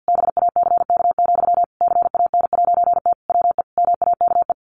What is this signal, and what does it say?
Morse code, heard as a keyed tone, about 50 wpm, 700 Hz. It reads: BACK, CAN'T WE MAKE
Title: A Sound Of Thunder 50wpm